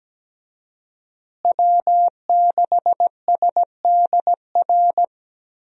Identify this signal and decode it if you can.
Morse code, heard as a keyed tone, about 17 wpm, 700 Hz. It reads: W6SDR
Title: USA Call Signs - 5 Characters - Repeated 2x - 17wpm